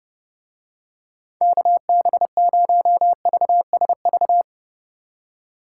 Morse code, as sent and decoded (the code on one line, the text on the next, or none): -.- -... ----- ...- ... ...-
KB0VSV